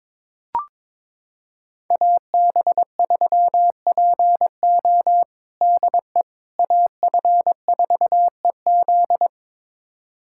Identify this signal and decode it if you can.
Morse code, heard as a keyed tone, about 22 wpm, 700 Hz. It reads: AB3PO DE AF4EZ
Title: USA Callsign DE Callsign - 5 Characters - 22wpm